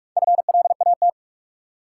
Morse code, heard as a keyed tone, about 46 wpm, 700 Hz. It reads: FLAT
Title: Four Letter Words - 45wpm